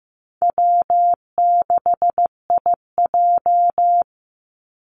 Morse code, heard as a keyed tone, about 15 wpm, 700 Hz. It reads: W6IJ
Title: Easy Call Signs Repeated 2x - 15wpm